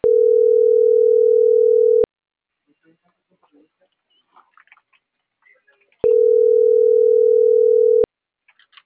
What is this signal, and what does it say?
Ringback tone (North America), 440 + 480 Hz.